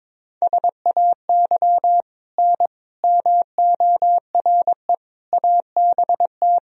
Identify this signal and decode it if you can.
Morse code, heard as a keyed tone, about 22 wpm, 700 Hz. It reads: SAY N MORE ABT
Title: Sentences from Top 200 Words and Abbreviations 22wpm